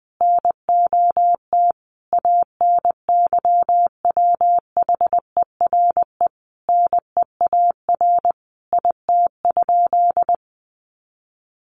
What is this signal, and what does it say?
Morse code, heard as a keyed tone, about 20 wpm, 700 Hz. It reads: NOT ANYWHERE NEAR IT?
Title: Sentences from Top 1400 Words 20wpm